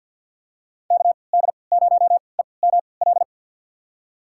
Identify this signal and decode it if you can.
Morse code, heard as a keyed tone, about 51 wpm, spaced as for 28 wpm, 700 Hz. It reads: KD0EML
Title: USA Call Signs - 3x Character Spacing - 50wpm